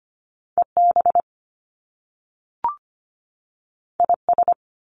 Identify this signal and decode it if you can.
Morse code, heard as a keyed tone, about 25 wpm, 700 Hz. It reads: EB IS
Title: Top Two Letter Combinations in Words 25wpm